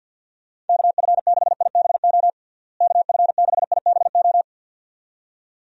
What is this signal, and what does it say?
Morse code, heard as a keyed tone, about 52 wpm, 700 Hz. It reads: KF6IBO KF6IBO
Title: USA Call Signs - 6 Characters - Repeated 2x - 50wpm